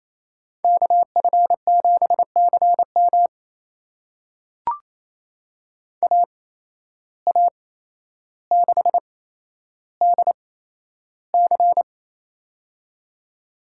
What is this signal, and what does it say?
Morse code, heard as a keyed tone, about 28 wpm, spaced as for 12 wpm, 700 Hz. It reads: KF7CM AA6DC
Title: USA Call Signs - 5 Characters - 8x Character Spacing - 28wpm